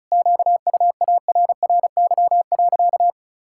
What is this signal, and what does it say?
Morse code, heard as a keyed tone, about 36 wpm, 700 Hz. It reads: QUARRY.
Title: Pangram Challenge v2 - 35wpm